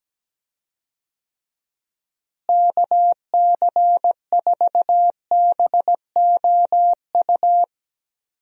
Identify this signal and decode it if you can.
Morse code, heard as a keyed tone, about 17 wpm, 700 Hz. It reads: KC4BOU
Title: USA Call Signs - 6 Characters - Repeated 2x - 17wpm